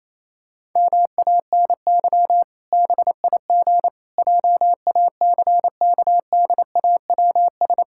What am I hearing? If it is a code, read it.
Morse code, heard as a keyed tone, about 28 wpm, 700 Hz. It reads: MANY BIG JACKDAWS
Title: Pangram Challenge v2 - 28wpm